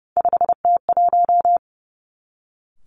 Morse code, heard as a keyed tone, about 30 wpm, 700 Hz. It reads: 5T1
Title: Three Challenging Letter-Number Combinations 30wpm